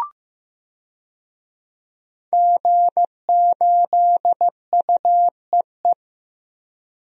Morse code, heard as a keyed tone, about 15 wpm, 700 Hz. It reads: G8UEE
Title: DXCC-100 Call Signs - 5 Characters - Repeated 3x - 15wpm